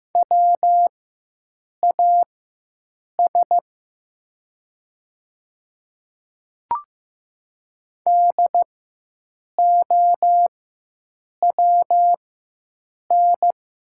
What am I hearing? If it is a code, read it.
Morse code, heard as a keyed tone, about 15 wpm, spaced as for 7 wpm, 700 Hz. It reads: WAS DOWN
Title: 4x Character Spacing - Top 100 Words - 15wpm